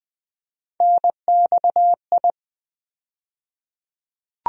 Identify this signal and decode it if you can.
Morse code, heard as a keyed tone, about 20 wpm, 700 Hz. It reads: NXI
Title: USA Call Sign Postfixes - 3 Letters 20wpm